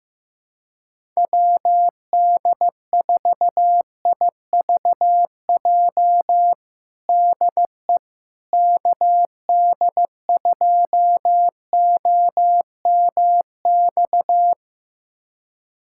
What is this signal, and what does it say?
Morse code, heard as a keyed tone, about 15 wpm, 700 Hz. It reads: WD4IVJ DE KD2OMX
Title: USA Callsign DE Callsign 15wpm